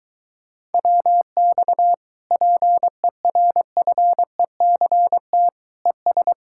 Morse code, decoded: WX PERFECT ES

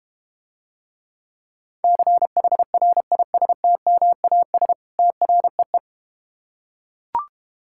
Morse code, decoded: CHRISTMAS TREE

